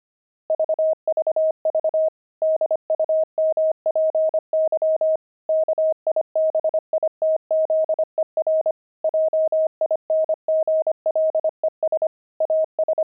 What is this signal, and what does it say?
Morse code, heard as a keyed tone, about 25 wpm, 630 Hz. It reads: VVV DUMPY KIBITZER JINGLES AS
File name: pangram-fs-25-cs-25-kibitzer